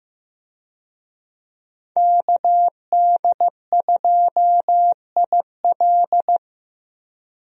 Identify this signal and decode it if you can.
Morse code, heard as a keyed tone, about 15 wpm, 700 Hz. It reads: KD2IL
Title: USA Call Signs - 5 Characters - Repeated 2x - 15wpm